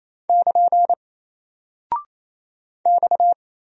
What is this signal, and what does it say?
Morse code, heard as a keyed tone, about 29 wpm, 700 Hz. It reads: <KN> X
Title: Single Character (Letter:Number:Punctuation:Prosign) Alt 28wpm